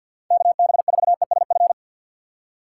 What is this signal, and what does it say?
Morse code, heard as a keyed tone, about 61 wpm, 700 Hz. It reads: KB4ESF
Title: Speed Racing - USA 2x3 Call Signs 50wpm